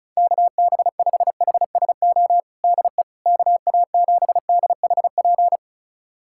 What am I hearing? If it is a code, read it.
Morse code, heard as a keyed tone, about 36 wpm, 700 Hz. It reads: KB5HSO DE KA7DHP
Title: USA Callsign DE Callsign 35wpm